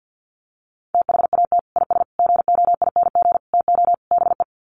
Morse code, heard as a keyed tone, about 48 wpm, 700 Hz. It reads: THAT IS GOING TO BE
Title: Sets of 5 Words Encore2 50wpm